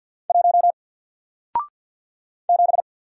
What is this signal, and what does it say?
Morse code, heard as a keyed tone, about 48 wpm, 700 Hz. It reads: 1 7
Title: Single Character (Letter:Number:Punctuation:Prosign) Alt 50wpm